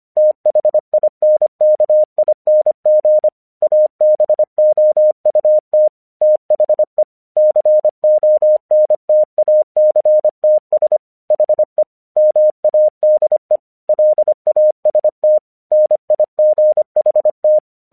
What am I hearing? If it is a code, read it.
Morse code, heard as a keyed tone, about 25 wpm, 610 Hz. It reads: THINKING ABOUT THE CONTACTS HE MADE LAST NIGHT
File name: 1 English 25wpm / AQC36